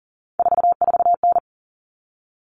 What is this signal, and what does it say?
Morse code, heard as a keyed tone, about 40 wpm, 700 Hz. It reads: 44N
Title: Common Cut Numbers - 3-Digit 40wpm